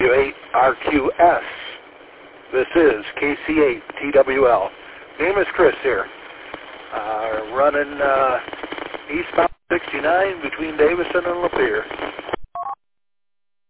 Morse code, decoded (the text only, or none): none